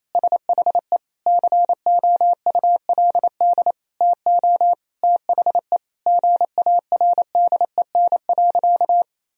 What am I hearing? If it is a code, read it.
Morse code, heard as a keyed tone, about 28 wpm, 700 Hz. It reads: SHE COULD TO THE GARDEN.
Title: Sentences from Top 1000 Words - Review Part 4 - 28wpm